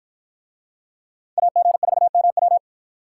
Morse code, heard as a keyed tone, about 53 wpm, 700 Hz. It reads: AG4MW